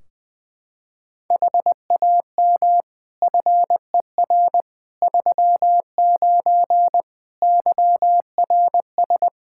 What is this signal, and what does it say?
Morse code, heard as a keyed tone, about 20 wpm, 700 Hz. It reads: HAM FER 39 YRS
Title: Practice QSO - Exchange 3 - More Info - 20wpm